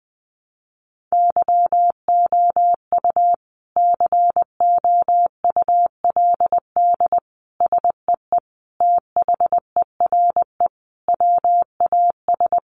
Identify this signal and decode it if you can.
Morse code, heard as a keyed tone, about 20 wpm, 700 Hz. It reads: YOU COULD SEE THERE WAS
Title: Sentences from Top 500 Words 20wpm